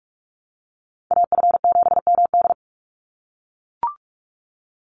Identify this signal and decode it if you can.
Morse code, heard as a keyed tone, about 44 wpm, 700 Hz. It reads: AF7MD
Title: NA SOTA Chaser Callsigns 45wpm